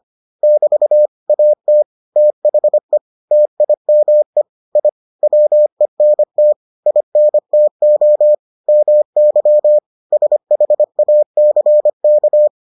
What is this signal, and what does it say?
Morse code, heard as a keyed tone, about 25 wpm, 600 Hz. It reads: <BT> AT THE TIME I WENT INTO MY SHACK